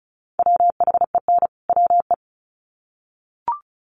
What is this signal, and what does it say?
Morse code, heard as a keyed tone, about 34 wpm, 700 Hz. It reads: WHEN WE
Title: Sets of 2 Words 35wpm